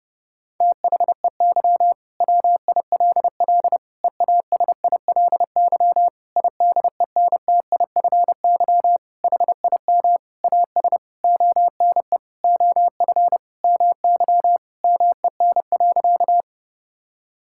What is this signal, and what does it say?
Morse code, heard as a keyed tone, about 30 wpm, 700 Hz. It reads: THEY WILL EASILY IDENTIFY HIM AS ONE OF MY MEN.
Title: Sentences from Top 1000 Words - Review Part 6 - 30wpm